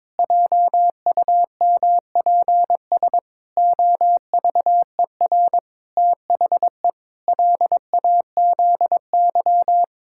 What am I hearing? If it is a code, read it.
Morse code, heard as a keyed tone, about 22 wpm, 700 Hz. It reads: JUMPS OVER THE LAZY
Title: Pangram Challenge v2 - 22wpm